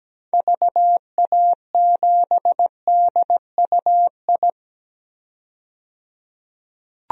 Morse code, heard as a keyed tone, about 17 wpm, 700 Hz. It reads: VA7DUI
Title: DXCC-100 Call Signs - 6 Characters - 17wpm